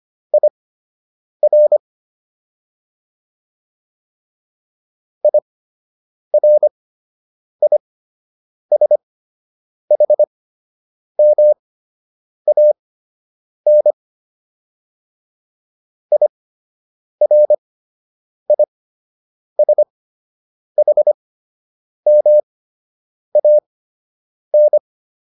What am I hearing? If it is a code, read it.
Morse code, heard as a keyed tone, about 25 wpm, spaced as for 8 wpm, 600 Hz. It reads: IR IRISHMAN IRISHMAN